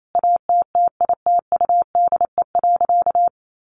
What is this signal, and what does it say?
Morse code, heard as a keyed tone, about 28 wpm, 700 Hz. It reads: ATTITUDE.
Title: Sentences from Top 1300 Words 28wpm